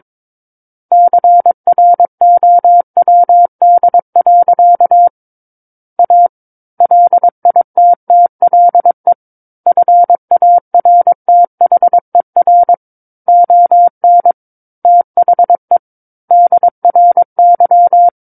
Morse code, heard as a keyed tone, about 22 wpm, 700 Hz. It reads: CROWD. A LITTLE FARTHER ON THE DRY
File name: Book One / 13-Chapter 13 - 22 WPM